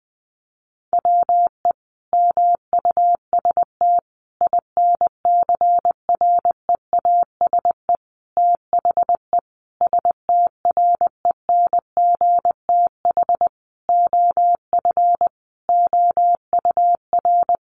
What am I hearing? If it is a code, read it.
Morse code, heard as a keyed tone, about 20 wpm, 700 Hz. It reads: WE MUST INCREASE THE STRENGTH OF OUR
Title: Sentences from Top 1300 Words 20wpm